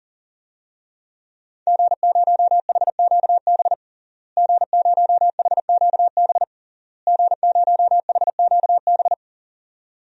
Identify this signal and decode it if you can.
Morse code, heard as a keyed tone, about 40 wpm, 700 Hz. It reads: G0HQB G0HQB G0HQB
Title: DXCC-100 Call Signs - 5 Characters - Repeated 3x - 40wpm